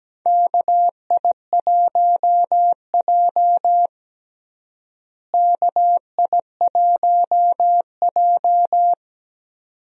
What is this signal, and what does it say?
Morse code, heard as a keyed tone, about 17 wpm, 700 Hz. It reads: KI1J KI1J
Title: USA Call Signs - 4 Characters - Repeated 2x - 17wpm